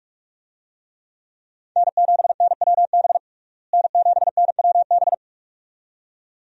Morse code, heard as a keyed tone, about 47 wpm, 700 Hz. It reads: N7NWB N7NWB
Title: USA Call Signs - 5 Characters - Repeated 2x - 45wpm